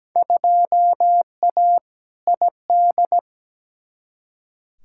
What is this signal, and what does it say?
Morse code, heard as a keyed tone, about 17 wpm, 700 Hz. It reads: 2A ID
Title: Field Day Exchanges - 17wpm